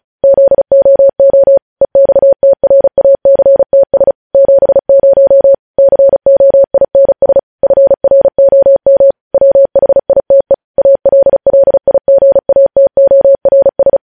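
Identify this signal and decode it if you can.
Morse code, heard as a keyed tone, about 36 wpm, 550 Hz. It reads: ZOO EXTRACTS 70 COINS FROM WHITE ALLIGATORS
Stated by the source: Morse code news headlines sent at 35 words per minute.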